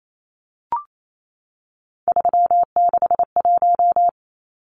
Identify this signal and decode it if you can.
Morse code, heard as a keyed tone, about 28 wpm, 700 Hz. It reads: 361
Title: Numbers - 3-Digit 28wpm